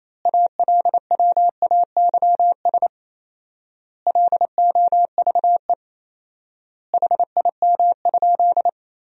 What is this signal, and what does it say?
Morse code, heard as a keyed tone, about 28 wpm, 700 Hz. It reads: ALWAYS LOVE HIM?
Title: 4x Word-Spacing - Sentences from Top 400 Words - 28wpm